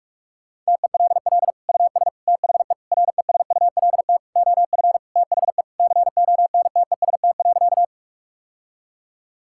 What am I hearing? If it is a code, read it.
Morse code, heard as a keyed tone, about 46 wpm, 700 Hz. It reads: TELL US THE RESULT OF THE CONTEST.
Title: Sentences from Top 1600 Words 45wpm